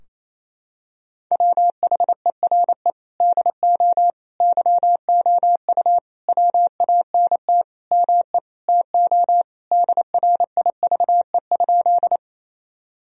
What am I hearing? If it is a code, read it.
Morse code, heard as a keyed tone, about 28 wpm, 700 Hz. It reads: WHERE DO YOU WANT ME TO DRIVE?
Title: Sentences from Top 1000 Words - Review Part 2 - 28wpm